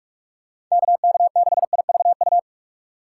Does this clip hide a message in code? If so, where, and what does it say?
Yes: Morse code from 0.7 to 2.4 s, heard as a keyed tone, about 44 wpm, 700 Hz. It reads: KK6IVU